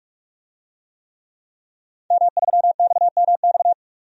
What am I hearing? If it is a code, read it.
Morse code, heard as a keyed tone, about 46 wpm, 700 Hz. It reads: M3XMX
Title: DXCC-100 Call Signs - 5 Characters - 45wpm